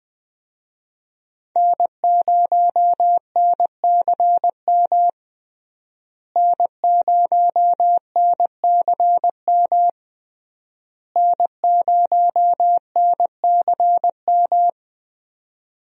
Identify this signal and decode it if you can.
Morse code, heard as a keyed tone, about 20 wpm, 700 Hz. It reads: N0NCM N0NCM N0NCM
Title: USA Call Signs - 5 Characters - Repeated 3x - 20wpm